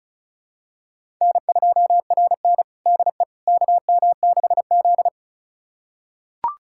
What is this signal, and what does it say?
Morse code, heard as a keyed tone, about 36 wpm, 690 Hz. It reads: N2RN DE KM6Z
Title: USA Callsign DE Callsign - 4 Characters - 35wpm